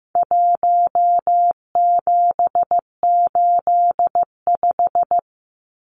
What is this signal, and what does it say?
Morse code, heard as a keyed tone, about 15 wpm, 700 Hz. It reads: 1785
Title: Numbers - 4-Digit 15wpm